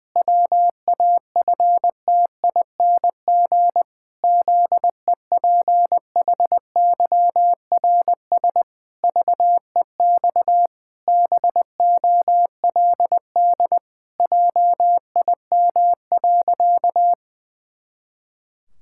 Morse code, heard as a keyed tone, about 20 wpm, 700 Hz. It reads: WAFTING ZEPHYRS VEX BOLD JIM.
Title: Pangram Challenge v2 - 20wpm